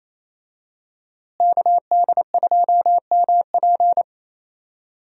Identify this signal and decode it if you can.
Morse code, heard as a keyed tone, about 27 wpm, 700 Hz. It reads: KD2MP
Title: USA Call Signs - 5 Characters - Repeated 2x - 28wpm